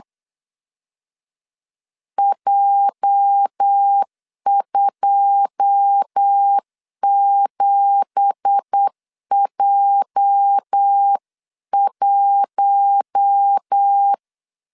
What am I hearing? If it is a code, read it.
Morse code, heard as a keyed tone, about 8 wpm, 800 Hz. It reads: J27J1
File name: CW / DET / 1 9 1500C S B 8wpm 5624